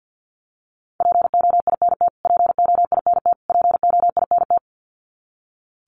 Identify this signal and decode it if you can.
Morse code, heard as a keyed tone, about 50 wpm, 700 Hz. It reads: POINT POINT POINT
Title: Top 500 Words Repeated 50wpm